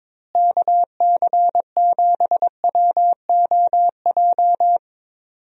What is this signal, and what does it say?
Morse code, heard as a keyed tone, about 22 wpm, 700 Hz. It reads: KC7WOJ
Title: USA Call Signs - 6 Characters - Repeated 3x - 22wpm